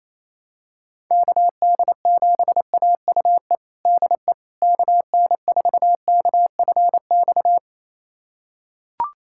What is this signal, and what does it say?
Morse code, heard as a keyed tone, about 28 wpm, 700 Hz. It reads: KD7AUE DE KN4KFX
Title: USA Callsign DE Callsign - 6 Characters - 28wpm